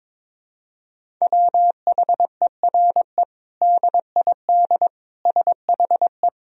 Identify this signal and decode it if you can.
Morse code, heard as a keyed tone, about 22 wpm, 700 Hz. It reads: WHERE DID SHE
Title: Sentences from Top 200 Words and Abbreviations 22wpm